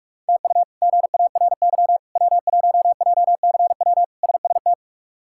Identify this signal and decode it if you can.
Morse code, heard as a keyed tone, about 46 wpm, 700 Hz. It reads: TU GARY W1JCW SST
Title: K1USN SST QSO Practice - 45wpm